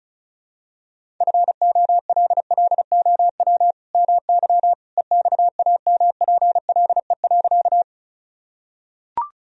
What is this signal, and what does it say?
Morse code, heard as a keyed tone, about 34 wpm, 700 Hz. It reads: FOLLOW MY EXAMPLE.
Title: Sentences from Top 1000 Words - Review Part 1 - 35wpm